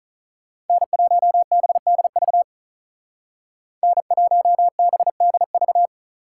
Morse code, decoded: N1BDV N1BDV

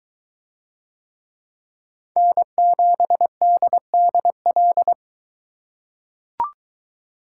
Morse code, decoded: N7DDL

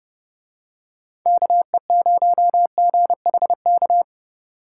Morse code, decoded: KE0GHK

